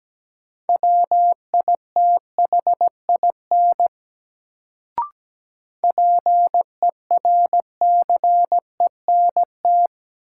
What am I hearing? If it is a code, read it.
Morse code, heard as a keyed tone, about 17 wpm, 700 Hz. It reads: WITHIN PERCENT
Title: Sets of 2 Words - New Word Drill - 17wpm